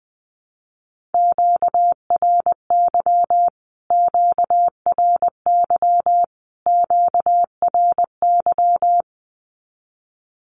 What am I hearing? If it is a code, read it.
Morse code, heard as a keyed tone, about 20 wpm, 700 Hz. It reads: QRY QRY QRY
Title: Q-Codes Repeated 3x 20wpm